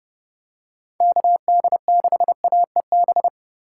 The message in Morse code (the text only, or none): KD6AEB